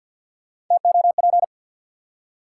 Morse code, heard as a keyed tone, about 48 wpm, 700 Hz. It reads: TOP
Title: Three Letter Words - 50wpm